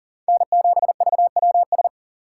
Morse code, heard as a keyed tone, about 40 wpm, 690 Hz. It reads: N7VWS
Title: USA Call Signs - 4x Character Spacing - 40wpm